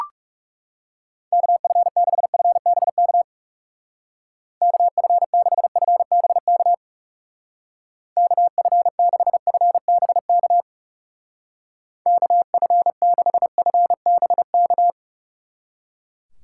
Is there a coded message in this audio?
Yes: Morse code from 1.3 to 14.9 s, heard as a keyed tone, about 38 wpm, 700 Hz. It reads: KF6FBK KF6FBK KF6FBK KF6FBK